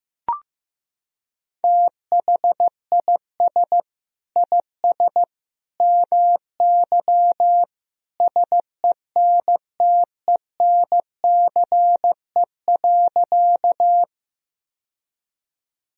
Morse code, heard as a keyed tone, about 15 wpm, 700 Hz. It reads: THIS IS MY SENTENCE.
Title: Sentences from Top 1000 Words - Review Part 1 - 15wpm